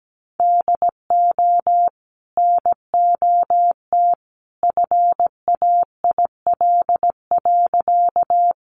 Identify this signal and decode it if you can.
Morse code, heard as a keyed tone, about 17 wpm, 700 Hz. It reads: DO NOT FAIL.
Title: Sentences from Top 1100 Words 17wpm